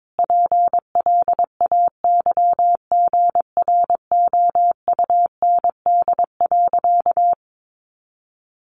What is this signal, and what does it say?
Morse code, heard as a keyed tone, about 22 wpm, 700 Hz. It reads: PLAYGROUND.
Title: Additional Easy Phrases 22wpm